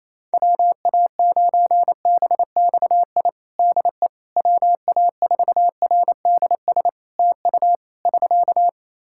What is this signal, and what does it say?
Morse code, heard as a keyed tone, about 28 wpm, 700 Hz. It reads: WA9BXI DE WA4RDS TU <SK>
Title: Practice QSO - Exchange 4 - Ending - 28wpm